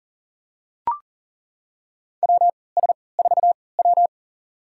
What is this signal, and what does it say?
Morse code, heard as a keyed tone, about 40 wpm, spaced as for 23 wpm, 700 Hz. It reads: WS4W
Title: USA Call Signs - 3x Character Spacing - 40wpm